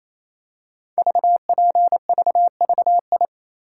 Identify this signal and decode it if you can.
Morse code, heard as a keyed tone, about 29 wpm, 700 Hz. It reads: VPVVI
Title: Code Groups - 5 Letters - 28wpm